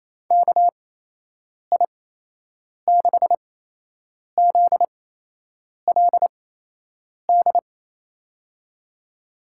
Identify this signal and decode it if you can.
Morse code, heard as a keyed tone, about 28 wpm, spaced as for 8 wpm, 700 Hz. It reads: KI6ZLD
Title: USA Call Signs - 6 Characters - 8x Character Spacing - 28wpm